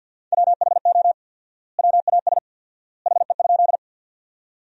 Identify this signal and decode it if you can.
Morse code, heard as a keyed tone, about 50 wpm, 700 Hz. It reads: WHO WAS HE?
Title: 4x Word-Spacing - Sentences from Top 100 Words - 50wpm